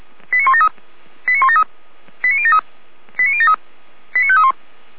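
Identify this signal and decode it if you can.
Five-tone selective call (ZVEI), heard as quick runs of five tones, tones of about 70 ms. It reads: 89173 89173 89073 89073 89531
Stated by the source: ZVEI five-tone selective call received on an SDR.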